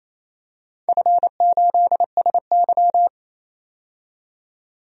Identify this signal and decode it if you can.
Morse code, heard as a keyed tone, about 28 wpm, 700 Hz. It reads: F8SY
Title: DXCC-100 Call Signs - 4 Characters - 28wpm